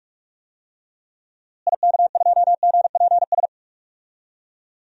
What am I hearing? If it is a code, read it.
Morse code, heard as a keyed tone, about 46 wpm, 690 Hz. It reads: IK2GPS
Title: DXCC-100 Call Signs - 6 Characters - Repeated 2x - 45wpm